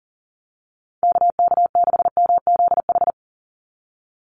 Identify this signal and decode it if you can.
Morse code, heard as a keyed tone, about 40 wpm, 700 Hz. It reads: KK6MZH
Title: USA Call Signs 40wpm